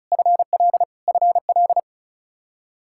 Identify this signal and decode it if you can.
Morse code, heard as a keyed tone, about 34 wpm, 700 Hz. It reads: FL FL
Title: Practice QSO - Exchange 1 - Beginning - 35wpm